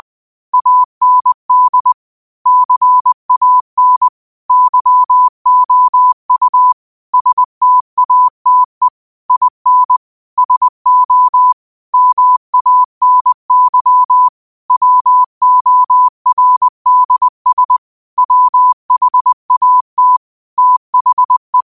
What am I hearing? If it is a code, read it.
Morse code, heard as a keyed tone, about 20 wpm, 1000 Hz. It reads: AND CAN YOU STATE IN SO MANY WORDS WHAT THE
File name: qotdfiles / 20wpm